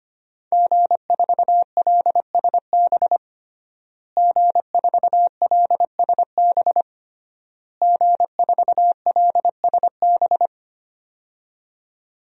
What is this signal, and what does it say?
Morse code, heard as a keyed tone, about 25 wpm, 700 Hz. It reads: G4LSB G4LSB G4LSB
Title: DXCC-100 Call Signs - 5 Characters - Repeated 3x - 25wpm